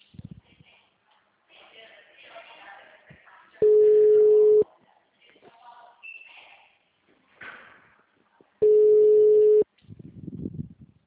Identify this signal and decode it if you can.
Ringback tone (Europe, 425 Hz), with speech.